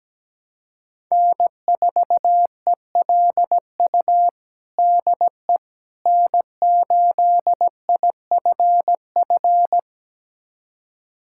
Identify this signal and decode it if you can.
Morse code, heard as a keyed tone, about 17 wpm, 700 Hz. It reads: N4ELU DE N8IFF
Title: USA Callsign DE Callsign - 5 Characters - 17wpm